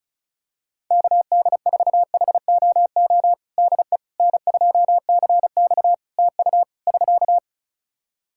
Morse code, heard as a keyed tone, about 36 wpm, 700 Hz. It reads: KD4HOO DE N2CX TU <SK>
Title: Practice QSO - Exchange 4 - Ending - 35wpm